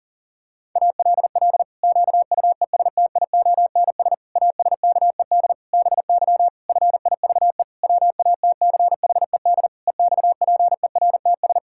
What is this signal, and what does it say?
Morse code, heard as a keyed tone, about 40 wpm, 700 Hz. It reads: ALL QUESTIONS ASKED BY FIVE WATCHED EXPERTS
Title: Pangram Challenge v2 - 40wpm